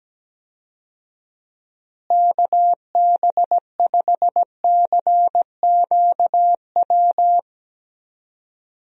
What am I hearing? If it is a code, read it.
Morse code, heard as a keyed tone, about 17 wpm, 700 Hz. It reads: KB5CQW
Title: USA Call Signs - 6 Characters - Repeated 2x - 17wpm